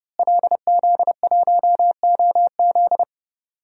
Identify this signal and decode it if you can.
Morse code, heard as a keyed tone, about 30 wpm, 700 Hz. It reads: LZ1OZ
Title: DXCC-100 Call Signs - 5 Characters - Repeated 2x - 30wpm